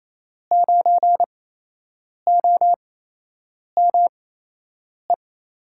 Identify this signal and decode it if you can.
Morse code, heard as a keyed tone, about 27 wpm, spaced as for 8 wpm, 700 Hz. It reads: 9OME
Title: USA Call Signs - 6 Characters - 8x Character Spacing - 28wpm